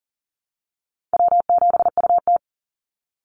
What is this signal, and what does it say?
Morse code, heard as a keyed tone, about 40 wpm, 700 Hz. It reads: W7UT
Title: Easy Call Signs Repeated 2x - 40wpm